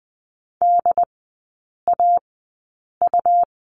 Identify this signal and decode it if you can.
Morse code, heard as a keyed tone, about 20 wpm, spaced as for 8 wpm, 700 Hz. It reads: DAU
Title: 121 Lesson 20e - Number 3 - Three Characters 20wpm